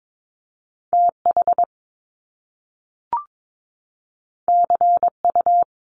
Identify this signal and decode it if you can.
Morse code, heard as a keyed tone, about 22 wpm, 700 Hz. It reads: TH CU
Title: Top Two Letter Combinations in Words 22wpm